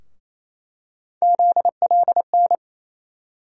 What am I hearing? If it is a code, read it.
Morse code, heard as a keyed tone, about 28 wpm, 700 Hz. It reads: ZLN
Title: USA Call Sign Postfixes - 3 Letters 28wpm